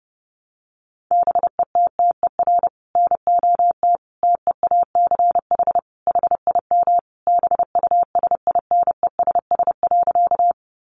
Morse code, heard as a keyed tone, about 30 wpm, 700 Hz. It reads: BETTER NOT TEACH HIM BUSINESS.
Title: Sentences from Top 600 Words 30wpm